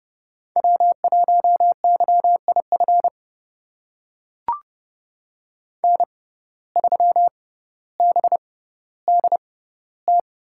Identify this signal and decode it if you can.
Morse code, heard as a keyed tone, about 29 wpm, spaced as for 15 wpm, 700 Hz. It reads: W1YIF N3BDT
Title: USA Call Signs - 5 Characters - 6x Character Spacing - 30wpm